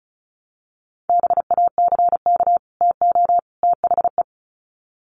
Morse code, heard as a keyed tone, about 34 wpm, 700 Hz. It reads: BACK TO THE
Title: Sets of 3 Words 35wpm